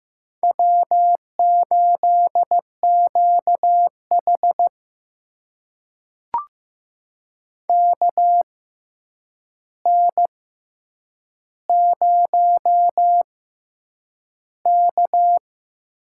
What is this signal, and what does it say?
Morse code, heard as a keyed tone, about 15 wpm, spaced as for 8 wpm, 700 Hz. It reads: W8QH KN0K
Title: USA Call Signs - 4 Characters - 6x Character Spacing - 15wpm